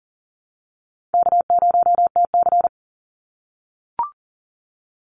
Morse code, heard as a keyed tone, about 40 wpm, 700 Hz. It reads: K0TC
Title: Easy Call Signs - 40wpm